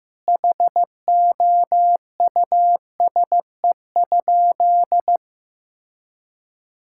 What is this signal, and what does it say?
Morse code, heard as a keyed tone, about 15 wpm, 700 Hz. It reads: HOUSE?
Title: Sentences from Top 1000 Words - Review Part 6 - 15wpm